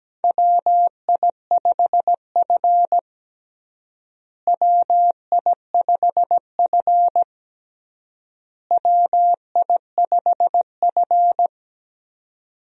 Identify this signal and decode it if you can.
Morse code, heard as a keyed tone, about 17 wpm, 700 Hz. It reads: WI5F WI5F WI5F
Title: USA Call Signs - 4 Characters - Repeated 3x - 17wpm